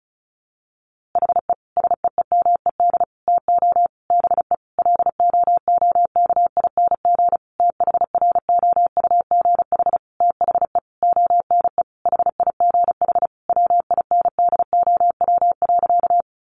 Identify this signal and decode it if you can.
Morse code, heard as a keyed tone, about 34 wpm, 700 Hz. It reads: HE SEEMED TO BE LOOKING THROUGH THE ONE HIGH WINDOW.
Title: A Sound Of Thunder 35wpm